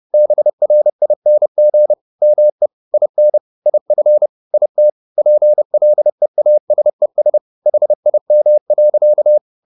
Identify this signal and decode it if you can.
Morse code, heard as a keyed tone, about 30 wpm, 600 Hz. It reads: BRING ME IN IF IT PLEASES HIM.